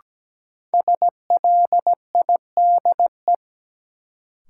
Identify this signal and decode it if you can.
Morse code, heard as a keyed tone, about 17 wpm, 700 Hz. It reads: SLIDE
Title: Sentences from Top 1600 Words - New Word Drill Alt - 17wpm